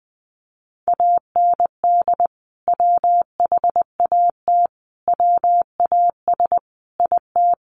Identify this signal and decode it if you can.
Morse code, heard as a keyed tone, about 20 wpm, 700 Hz. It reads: AND WHAT WAS IT
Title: Sentences from Top 100 Words 20wpm